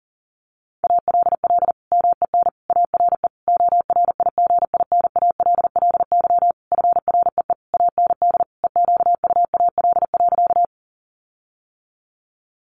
Morse code, heard as a keyed tone, about 40 wpm, 700 Hz. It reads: ALL MEN ARE ORIGINALLY FREE AND EQUAL.
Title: Sentences from Top 1500 Words 40wpm